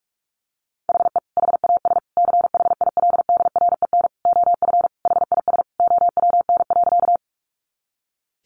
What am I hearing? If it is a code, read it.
Morse code, heard as a keyed tone, about 44 wpm, 700 Hz. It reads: HE HAS CHILDREN OF HIS OWN.
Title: Sentences from Top 300 Words 45wpm